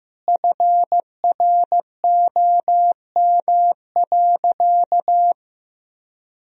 Morse code, heard as a keyed tone, about 15 wpm, 700 Hz. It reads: FROM.
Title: Sentences from Top 1000 Words - Review Part 4 - 15wpm